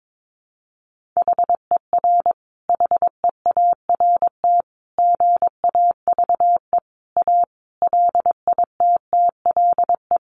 Morse code, heard as a keyed tone, about 22 wpm, 700 Hz. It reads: HER HEART GAVE A LITTLE
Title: Sentences from Top 1300 Words 22wpm